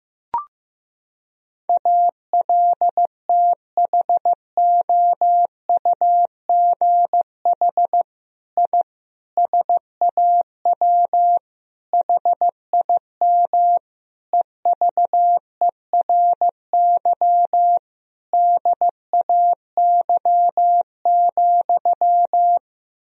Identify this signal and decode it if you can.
Morse code, heard as a keyed tone, about 16 wpm, 700 Hz. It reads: ALTHOUGH I SAW HIM EVERY DAY,
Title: Sentences from Top 1000 Words - Review Part 3 - 15wpm